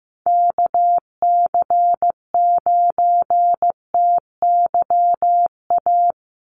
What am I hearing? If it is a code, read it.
Morse code, heard as a keyed tone, about 15 wpm, 700 Hz. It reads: KC9TYA
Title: NA SOTA Chaser Callsigns 15wpm